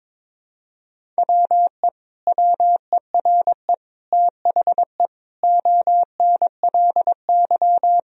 Morse code, heard as a keyed tone, about 22 wpm, 700 Hz. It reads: WE WERE THE ONLY
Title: Sentences from Top 1000 Words - Review Part 1 - 22wpm